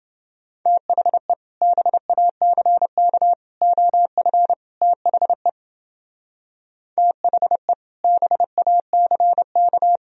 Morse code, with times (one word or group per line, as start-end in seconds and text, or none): 0.7-1.3 THE
1.6-3.3 BACK
3.6-4.5 OF
4.8-5.5 THE
7.0-7.7 THE
8.0-10.0 BACK